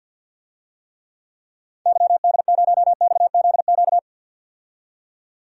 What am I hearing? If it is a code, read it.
Morse code, heard as a keyed tone, about 48 wpm, 700 Hz. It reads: YD0XZQ
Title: DXCC-100 Call Signs - 6 Characters - 50wpm